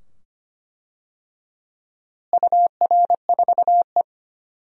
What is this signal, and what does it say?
Morse code, heard as a keyed tone, about 25 wpm, 700 Hz. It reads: UR4E
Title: DXCC-100 Call Signs - 4 Characters - Repeated 2x - 25wpm